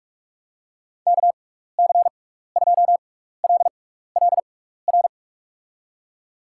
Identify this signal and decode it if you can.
Morse code, heard as a keyed tone, about 42 wpm, spaced as for 15 wpm, 700 Hz. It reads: KC2LLR
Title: USA Call Signs - 6 Characters - 6x Character Spacing - 45wpm